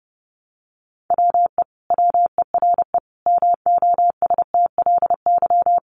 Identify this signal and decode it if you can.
Morse code, heard as a keyed tone, about 30 wpm, 700 Hz. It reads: WE WERE MOSTLY
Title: Sentences from Top 1300 Words 30wpm